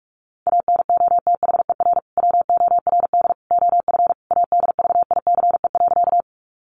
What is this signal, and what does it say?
Morse code, heard as a keyed tone, about 46 wpm, 700 Hz. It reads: ANOTHER WORD OF ADVICE.
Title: Sentences from Top 1100 Words 45wpm